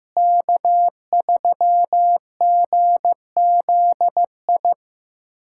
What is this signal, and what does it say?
Morse code, heard as a keyed tone, about 15 wpm, 700 Hz. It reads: K3GZI
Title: USA Callsign DE Callsign - 5 Characters - 15wpm